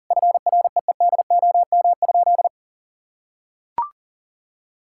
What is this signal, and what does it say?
Morse code, heard as a keyed tone, about 40 wpm, 700 Hz. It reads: FREEDOM?
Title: Sentences from Top 1600 Words 40wpm